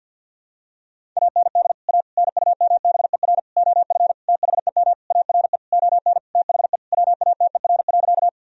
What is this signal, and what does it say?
Morse code, heard as a keyed tone, about 50 wpm, 700 Hz. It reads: AND A NUMBER OF THEM ARE ON THE WATER.
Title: Sentences from Top 1000 Words - Review Part 1 - 50wpm